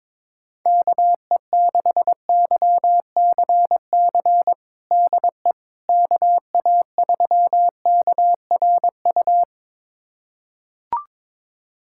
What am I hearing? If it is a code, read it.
Morse code, heard as a keyed tone, about 22 wpm, 700 Hz. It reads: KE6YCC DE KA3KRU
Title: USA Callsign DE Callsign 22wpm